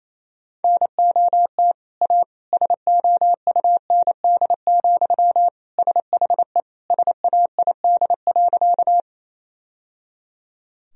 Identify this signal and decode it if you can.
Morse code, heard as a keyed tone, about 28 wpm, 700 Hz. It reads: NOT A SOUND, SHE SAID.
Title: Sentences from Top 1000 Words - Review Part 1 - 28wpm